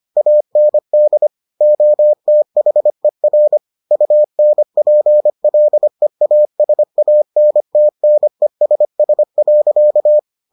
Morse code, heard as a keyed tone, about 25 wpm, 600 Hz. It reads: AND OTHER UNPLEASANTNESS.